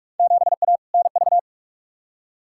Morse code, heard as a keyed tone, about 46 wpm, 700 Hz. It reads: 7A NV
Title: Field Day Exchanges Encore - 45wpm